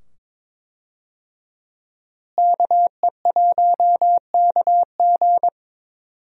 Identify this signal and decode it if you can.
Morse code, heard as a keyed tone, about 21 wpm, 700 Hz. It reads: KE1KG
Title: USA Call Signs - 5 Characters - Repeated 2x - 22wpm